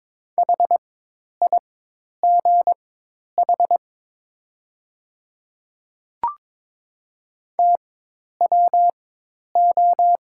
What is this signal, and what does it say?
Morse code, heard as a keyed tone, about 23 wpm, spaced as for 10 wpm, 700 Hz. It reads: HIGH TWO
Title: 4x Character Spacing - Top 200 Words - 22wpm